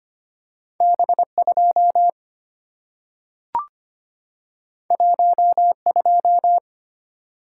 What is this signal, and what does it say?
Morse code, heard as a keyed tone, about 25 wpm, 700 Hz. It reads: B2 12
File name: Two Letter-Number Combinations - 25wpm